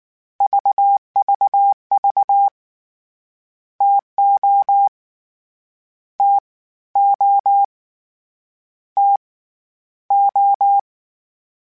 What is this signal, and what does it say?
Morse code, heard as a keyed tone, about 19 wpm, 800 Hz. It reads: VVV TO T O T O
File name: Variable Spacing / Variable Spacing-mp3s / TOs @18